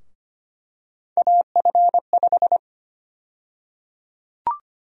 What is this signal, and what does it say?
Morse code, heard as a keyed tone, about 25 wpm, 700 Hz. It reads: AF5
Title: USA Call Sign Prefixes - Repeated 2x - 25wpm